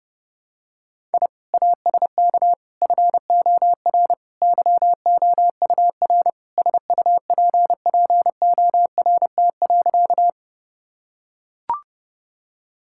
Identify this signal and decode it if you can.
Morse code, heard as a keyed tone, about 30 wpm, 700 Hz. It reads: I ASK FOR YOUR SUPPORT.
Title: Sentences from Top 1000 Words - Review Part 5 - 30wpm